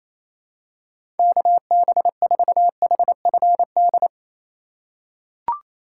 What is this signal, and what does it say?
Morse code, heard as a keyed tone, about 28 wpm, 700 Hz. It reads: KB4HFD
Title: USA Call Signs - 2x Character Spacing - 28wpm